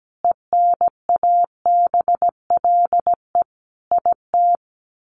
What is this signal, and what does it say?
Morse code, heard as a keyed tone, about 17 wpm, 700 Hz. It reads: ENABLE IT
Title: Sentences from Top 1500 Words 17wpm